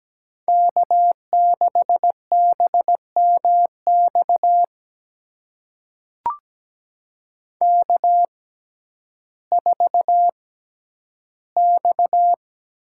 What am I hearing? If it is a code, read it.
Morse code, heard as a keyed tone, about 17 wpm, spaced as for 10 wpm, 700 Hz. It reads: K6BMX K4X
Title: USA Call Signs - 6x Character Spacing - 17wpm